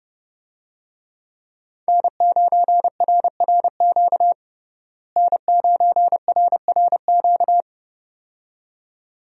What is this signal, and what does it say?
Morse code, heard as a keyed tone, about 30 wpm, 700 Hz. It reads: N9RRQ N9RRQ
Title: USA Call Signs - 5 Characters - Repeated 2x - 30wpm